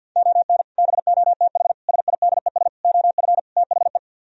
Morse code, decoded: ON BOTH SIDES OF THE

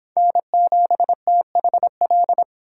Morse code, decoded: N7THL

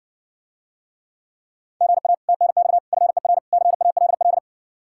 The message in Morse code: GA TNX FER CALL

